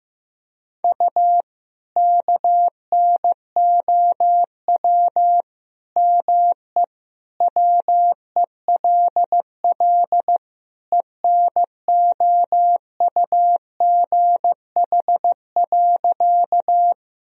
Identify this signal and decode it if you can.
Morse code, heard as a keyed tone, about 16 wpm, 700 Hz. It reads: U KNOW ME WELL ENOUGH.
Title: Sentences from Top 400 Words and Abbreviations 15wpm